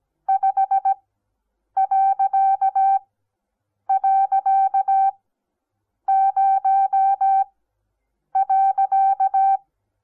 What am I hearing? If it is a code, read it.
Morse code, heard as a keyed tone, about 17 wpm, spaced as for 7 wpm, 780 Hz. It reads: 5..0.
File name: Onlinelehrgang / morsen / DJ4UF Morsen08